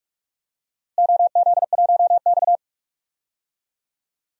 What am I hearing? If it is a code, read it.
Morse code, heard as a keyed tone, about 41 wpm, 700 Hz. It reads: OZ1X
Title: DXCC-100 Call Signs - 4 Characters - Repeated 2x - 45wpm